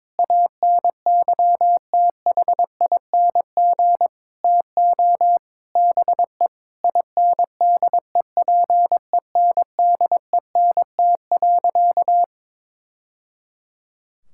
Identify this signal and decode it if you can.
Morse code, heard as a keyed tone, about 22 wpm, 700 Hz. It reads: ANYTHING TO BE INDEPENDENT.
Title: Sentences from Top 1000 Words - Review Part 6 - 22wpm